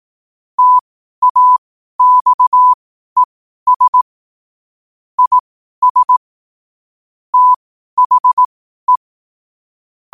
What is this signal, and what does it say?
Morse code, heard as a keyed tone, about 18 wpm, spaced as for 12 wpm, 1000 Hz. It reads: TAXES IS THE
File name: qotdfiles / 12wpm